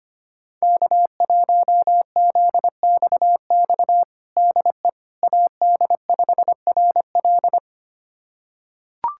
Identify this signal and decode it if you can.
Morse code, heard as a keyed tone, about 25 wpm, 700 Hz. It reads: K1ZXX DE AD5RL
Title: USA Callsign DE Callsign - 5 Characters - 25wpm